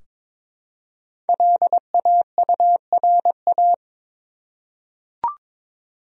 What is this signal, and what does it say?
Morse code, heard as a keyed tone, about 23 wpm, 700 Hz. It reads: LAURA
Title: USA Common Names - Max 5 Letters - 22wpm